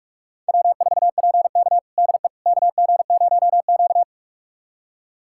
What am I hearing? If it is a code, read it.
Morse code, heard as a keyed tone, about 44 wpm, 700 Hz. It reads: W4PK DE KG0Q
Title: USA Callsign DE Callsign - 4 Characters - 45wpm